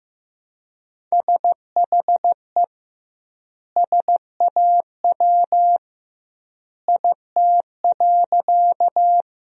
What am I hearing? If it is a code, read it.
Morse code, heard as a keyed tone, about 15 wpm, 700 Hz. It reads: SHE SAW IT.
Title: 2x Word-Spacing - Sentences from Top 300 Words - 15wpm